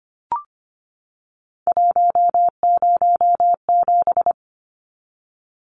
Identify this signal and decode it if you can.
Morse code, heard as a keyed tone, about 25 wpm, 700 Hz. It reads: 107
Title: Numbers - 3-Digit 25wpm